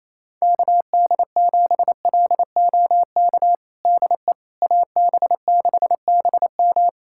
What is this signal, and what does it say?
Morse code, heard as a keyed tone, about 28 wpm, 700 Hz. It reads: KD7LOK DE AB6BM
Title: USA Callsign DE Callsign 28wpm